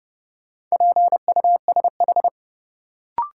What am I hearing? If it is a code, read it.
Morse code, heard as a keyed tone, about 30 wpm, 700 Hz. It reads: PUSH
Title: Sentences from Top 1300 Words - New Word Drill - 30wpm